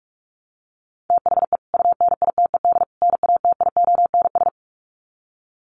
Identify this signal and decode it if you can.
Morse code, heard as a keyed tone, about 44 wpm, 700 Hz. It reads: THE UNITED NATIONS
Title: Sets of 3 Words 45wpm